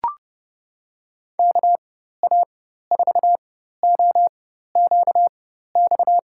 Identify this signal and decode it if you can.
Morse code, heard as a keyed tone, about 30 wpm, spaced as for 14 wpm, 700 Hz. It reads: KA4OQX
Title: USA Call Signs - 6 Characters - 4x Character Spacing - 30wpm